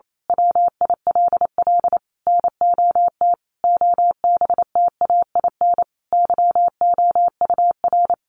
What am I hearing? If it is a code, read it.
Morse code, heard as a keyed tone, about 28 wpm, 700 Hz. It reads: WILL NOT OBTAIN YOUR
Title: Sentences from Top 1300 Words 28wpm